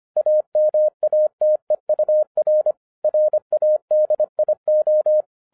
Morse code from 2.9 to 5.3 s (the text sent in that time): RADIO